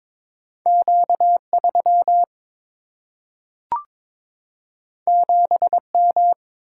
Morse code, heard as a keyed tone, about 22 wpm, 700 Hz. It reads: Q 3 7 M
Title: Single-Letter Single-Number 22wpm